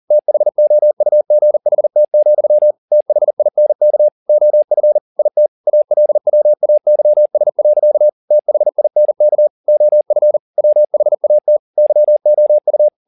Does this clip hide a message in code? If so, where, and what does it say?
Yes: Morse code from 0.1 to 12.9 s, heard as a keyed tone, about 40 wpm, 600 Hz. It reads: THOUGHT, THINK OF IT ALWAYS. THINK OF WHAT YOU